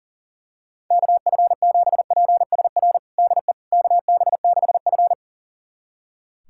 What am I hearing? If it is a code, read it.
Morse code, heard as a keyed tone, about 40 wpm, 700 Hz. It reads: KF7PSR DE KB6F
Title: USA Callsign DE Callsign 40wpm